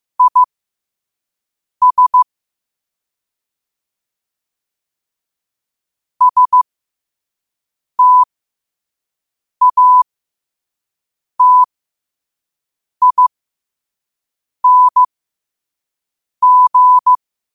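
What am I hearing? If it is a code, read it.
Morse code, heard as a keyed tone, about 15 wpm, spaced as for 5 wpm, 1000 Hz. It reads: IS STATING
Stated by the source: Quote-of-the-day Morse code practice at 5 wpm.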